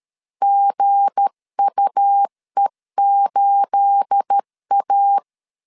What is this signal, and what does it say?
Morse code, heard as a keyed tone, about 13 wpm, 800 Hz. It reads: GUE8A
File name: LEZIONI / 9 28 12wpm 2351